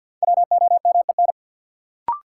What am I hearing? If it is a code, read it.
Morse code, heard as a keyed tone, about 52 wpm, 700 Hz. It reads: WOMEN
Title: Sentences from Top 1300 Words - New Word Drill - 50wpm